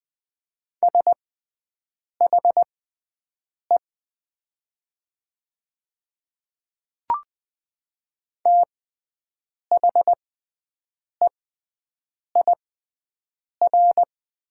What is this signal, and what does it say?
Morse code, heard as a keyed tone, about 20 wpm, spaced as for 7 wpm, 700 Hz. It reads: SHE THEIR
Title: 6x Character Spacing - Top 100 Words - 20wpm